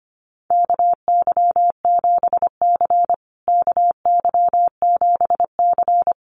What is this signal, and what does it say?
Morse code, heard as a keyed tone, about 25 wpm, 700 Hz. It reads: KY7C KY7C
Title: USA Call Signs Repeated 3x 25wpm